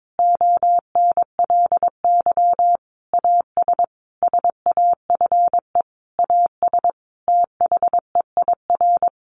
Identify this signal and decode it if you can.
Morse code, heard as a keyed tone, about 22 wpm, 700 Hz. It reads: ONLY AS SAFE AS THEIR
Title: Sentences from Top 1500 Words 22wpm